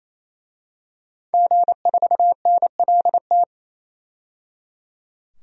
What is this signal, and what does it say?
Morse code, heard as a keyed tone, about 28 wpm, 700 Hz. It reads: G4NLT
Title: DXCC-100 Call Signs - 5 Characters - 28wpm